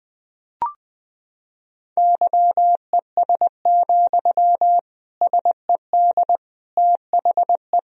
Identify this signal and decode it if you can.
Morse code, heard as a keyed tone, about 20 wpm, 700 Hz. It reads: YES, SED THE
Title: Sentences from Top 500 Words and Abbreviations 20wpm